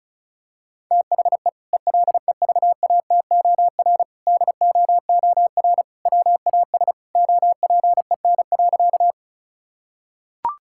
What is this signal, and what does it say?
Morse code, heard as a keyed tone, about 36 wpm, 700 Hz. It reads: THE ELEVATOR DOOR WAS OPEN.
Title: Sentences from Top 1600 Words 35wpm